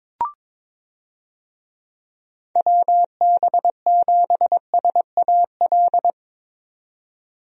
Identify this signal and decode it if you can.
Morse code, heard as a keyed tone, about 22 wpm, 700 Hz. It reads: WB7SAL
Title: USA Call Signs - 6 Characters - 22wpm